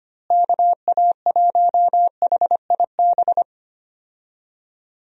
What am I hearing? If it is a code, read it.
Morse code, heard as a keyed tone, about 25 wpm, 700 Hz. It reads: KA1HIB
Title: USA Call Signs - 6 Characters - Repeated 2x - 25wpm